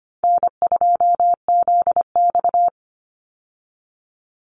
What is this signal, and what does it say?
Morse code, heard as a keyed tone, about 25 wpm, 700 Hz. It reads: N2ZX
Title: Easy Call Signs Repeated 2x - 25wpm